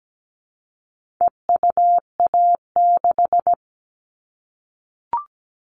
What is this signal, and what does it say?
Morse code, heard as a keyed tone, about 17 wpm, 700 Hz. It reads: EUA6
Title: All Cut Numbers - 4-Digit 17wpm